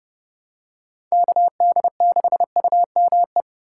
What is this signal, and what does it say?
Morse code, heard as a keyed tone, about 28 wpm, 700 Hz. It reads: KD6UME